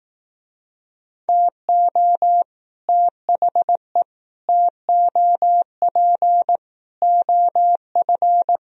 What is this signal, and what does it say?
Morse code, heard as a keyed tone, about 18 wpm, 700 Hz. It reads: TO THE TOP OF
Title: Speed-Racing - Sets of 4 Words 15wpm